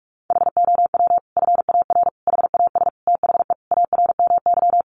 Morse code, heard as a keyed tone, about 44 wpm, 700 Hz. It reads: HOW FAR HAS THE ARMY
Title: Sentences from Top 1200 Words 45wpm